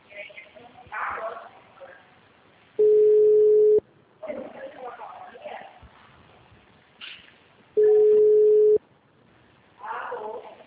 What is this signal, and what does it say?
Ringback tone (Europe, 425 Hz), with speech.
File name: out- -2025- - - .40251